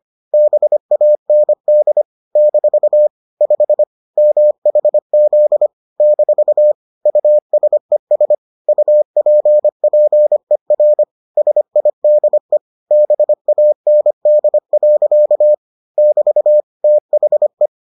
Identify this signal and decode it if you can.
Morse code, heard as a keyed tone, about 25 wpm, 600 Hz. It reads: BAND - 5 MHZ - USES UPPER SIDE BAND. <BT> THE